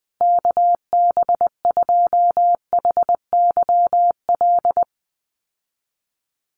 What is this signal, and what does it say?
Morse code, heard as a keyed tone, about 20 wpm, 700 Hz. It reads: KB2HYL
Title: USA Call Signs Repeated 2x 20wpm